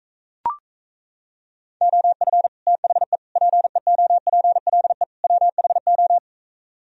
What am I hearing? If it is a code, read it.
Morse code, heard as a keyed tone, about 35 wpm, 700 Hz. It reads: OF THE PEOPLE WHO
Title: Speed-Racing - Sets of 4 Words 28wpm